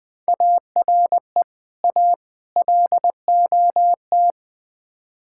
Morse code, heard as a keyed tone, about 21 wpm, 700 Hz. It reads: ARE A LOT
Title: Build A Phrase - 5-Word Phrases - 20wpm